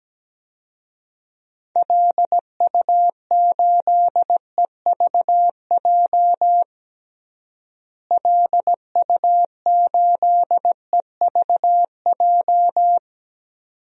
Morse code, heard as a keyed tone, about 17 wpm, 700 Hz. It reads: LU8EVJ LU8EVJ
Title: DXCC-100 Call Signs - 6 Characters - Repeated 2x - 17wpm